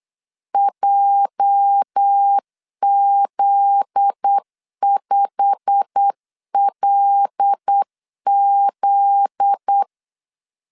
Morse code, heard as a keyed tone, about 8 wpm, 800 Hz. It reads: JZ5LZ
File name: LEZIONI / 9 2 8wpm 3437